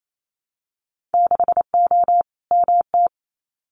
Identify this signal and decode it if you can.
Morse code, heard as a keyed tone, about 28 wpm, 700 Hz. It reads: 6O MT
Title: Winter Field Day Exchanges Encore 28wpm